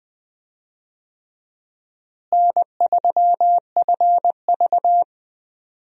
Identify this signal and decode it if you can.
Morse code, heard as a keyed tone, about 20 wpm, 700 Hz. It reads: N3FV
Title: USA Call Signs - 4 Characters - Repeated 3x - 20wpm